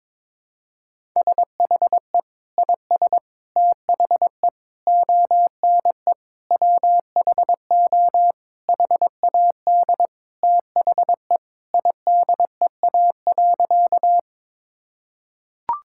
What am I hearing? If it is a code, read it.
Morse code, heard as a keyed tone, about 22 wpm, 700 Hz. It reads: SHE IS THE ONE WHO HAD THE IDEA.
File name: practice / 22wpm / Sentences with 2-Word Phrases - 22wpm